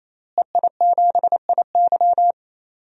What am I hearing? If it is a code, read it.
Morse code, heard as a keyed tone, about 28 wpm, 700 Hz. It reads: EI7IY
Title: DXCC-100 Call Signs - 5 Characters - Repeated 3x - 28wpm